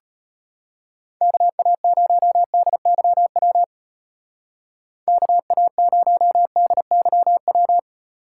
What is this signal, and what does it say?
Morse code, heard as a keyed tone, about 32 wpm, 700 Hz. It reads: KA0DYW KA0DYW